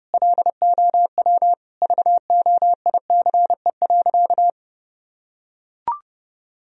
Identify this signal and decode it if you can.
Morse code, heard as a keyed tone, about 32 wpm, 700 Hz. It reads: LOW VOICE.
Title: Sentences from Top 1000 Words - Review Part 1 - 30wpm